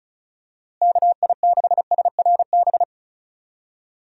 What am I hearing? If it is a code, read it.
Morse code, heard as a keyed tone, about 34 wpm, 700 Hz. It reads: KI6SRB
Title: USA Call Signs - 6 Characters - Repeated 3x - 35wpm